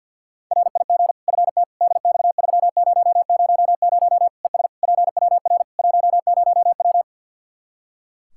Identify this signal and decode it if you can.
Morse code, heard as a keyed tone, about 49 wpm, 700 Hz. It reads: RIG FT DX3000 ES PWR 10W
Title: Practice QSO - Exchange 2 - Info - 50wpm